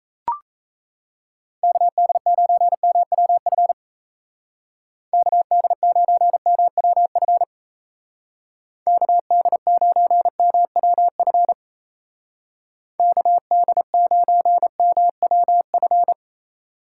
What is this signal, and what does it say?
Morse code, heard as a keyed tone, about 34 wpm, 700 Hz. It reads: KD9MWF KD9MWF KD9MWF KD9MWF
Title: Speed Racing - USA 2x3 Call Signs 28wpm